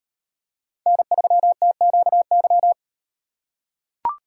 Morse code, heard as a keyed tone, about 37 wpm, 700 Hz. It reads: N3TQY